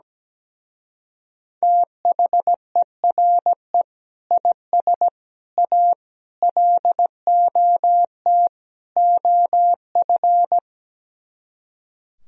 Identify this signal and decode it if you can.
Morse code, heard as a keyed tone, about 17 wpm, 700 Hz. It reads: THERE IS A LOT OF
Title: Speed-Racing - Sets of 5 Words 17wpm